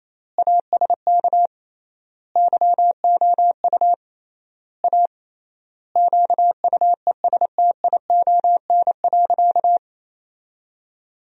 Morse code, heard as a keyed tone, about 28 wpm, 700 Hz. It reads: ASK YOU A QUESTION.
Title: 3x Word-Spacing - Sentences from Top 400 Words - 28wpm